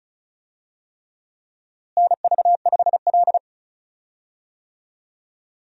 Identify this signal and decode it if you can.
Morse code, heard as a keyed tone, about 34 wpm, 700 Hz. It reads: NV5L
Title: USA Call Signs - 4 Characters - Repeated 3x - 35wpm